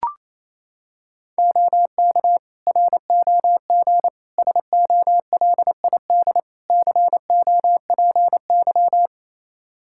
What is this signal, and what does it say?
Morse code, heard as a keyed tone, about 28 wpm, 700 Hz. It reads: OK ROG SOLID COPY
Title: Practice QSO - Exchange 3 - More Info - 28wpm